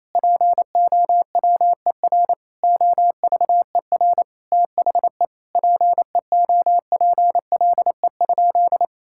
Morse code, heard as a keyed tone, about 28 wpm, 700 Hz. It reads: POWER OVER THE PEOPLE?
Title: Sentences from Top 1000 Words - Review Part 5 - 28wpm